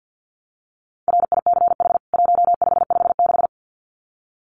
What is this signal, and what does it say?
Morse code, heard as a keyed tone, about 48 wpm, 700 Hz. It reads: RICH 1556
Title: CWT Exchange 50wpm